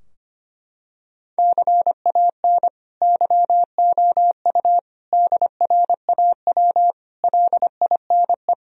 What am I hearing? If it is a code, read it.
Morse code, heard as a keyed tone, about 25 wpm, 700 Hz. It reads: CAN YOU DRAW LINE
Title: Sentences from Top 1000 Words - Review Part 3 - 25wpm